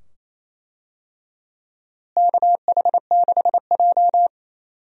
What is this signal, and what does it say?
Morse code, heard as a keyed tone, about 28 wpm, 700 Hz. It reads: KH6J
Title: DXCC-100 Call Signs - 4 Characters - Repeated 2x - 28wpm